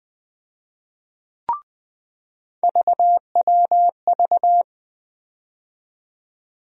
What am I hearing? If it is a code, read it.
Morse code, heard as a keyed tone, about 20 wpm, 700 Hz. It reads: VWV
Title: USA Call Sign Postfixes - 3 Letters 20wpm